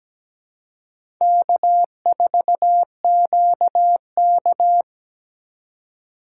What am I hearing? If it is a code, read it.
Morse code, heard as a keyed tone, about 17 wpm, 700 Hz. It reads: K4QK
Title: USA Call Signs - 4 Characters - 2x Character Spacing - 17wpm